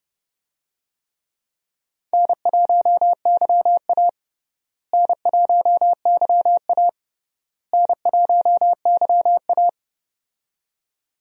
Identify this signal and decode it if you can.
Morse code, heard as a keyed tone, about 30 wpm, 700 Hz. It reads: N1YA N1YA N1YA
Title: USA Call Signs - 4 Characters - Repeated 3x - 30wpm